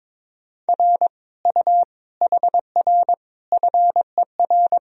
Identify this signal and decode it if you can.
Morse code, heard as a keyed tone, about 22 wpm, 700 Hz. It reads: R U HR FER
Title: Sentences from Top 200 Words and Abbreviations 22wpm